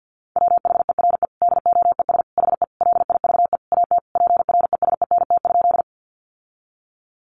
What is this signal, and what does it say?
Morse code, heard as a keyed tone, about 50 wpm, 700 Hz. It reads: WHERE DOES HE LIVE AT PRESENT?
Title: Sentences from Top 800 Words 50wpm